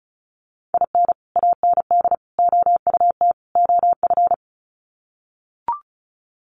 Morse code, heard as a keyed tone, about 34 wpm, 700 Hz. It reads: IN AND OUT OF
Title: Sets of 4 Words 35wpm